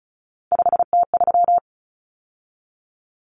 Morse code, heard as a keyed tone, about 34 wpm, 700 Hz. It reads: 5T3
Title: Common Cut Numbers - 3-Digit 35wpm